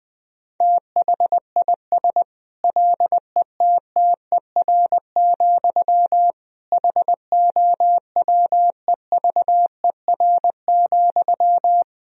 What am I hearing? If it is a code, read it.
Morse code, heard as a keyed tone, about 20 wpm, 700 Hz. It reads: THIS LETTER, HOWEVER,
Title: Sentences from Top 1000 Words - Review Part 4 - 20wpm